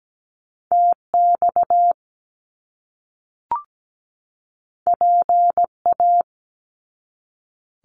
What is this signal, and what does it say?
Morse code, heard as a keyed tone, about 17 wpm, 700 Hz. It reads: TX PA
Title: US States 17wpm